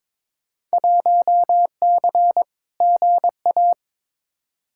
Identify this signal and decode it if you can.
Morse code, heard as a keyed tone, about 23 wpm, 700 Hz. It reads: 1C GA
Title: Field Day Exchanges - 22wpm